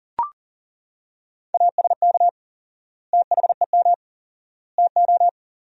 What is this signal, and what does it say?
Morse code, heard as a keyed tone, about 40 wpm, 700 Hz. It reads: ASK THEM TO
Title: 4x Word-Spacing - Sentences from Top 400 Words - 40wpm